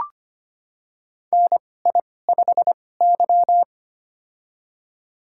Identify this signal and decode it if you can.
Morse code, heard as a keyed tone, about 25 wpm, spaced as for 18 wpm, 700 Hz. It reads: NI5Y
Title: USA Call Signs - 2x Character Spacing - 25wpm